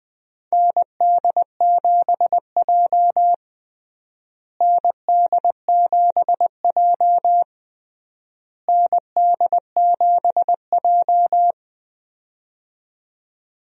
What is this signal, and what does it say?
Morse code, heard as a keyed tone, about 20 wpm, 700 Hz. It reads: ND7J ND7J ND7J
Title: USA Call Signs - 4 Characters - Repeated 3x - 20wpm